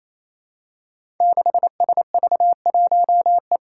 Morse code, heard as a keyed tone, about 27 wpm, 700 Hz. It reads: 6SV1E
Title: Five Challenging Letter-Number Combinations 28wpm